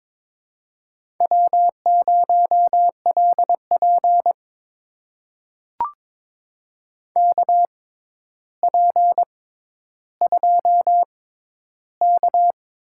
Morse code, heard as a keyed tone, about 22 wpm, spaced as for 11 wpm, 700 Hz. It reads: W0LP KP2K
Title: USA Call Signs - 4 Characters - 6x Character Spacing - 22wpm